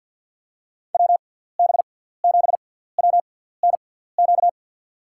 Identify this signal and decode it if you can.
Morse code, heard as a keyed tone, about 47 wpm, spaced as for 17 wpm, 700 Hz. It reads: WB7WNQ
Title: USA Call Signs - 6x Character Spacing - 50wpm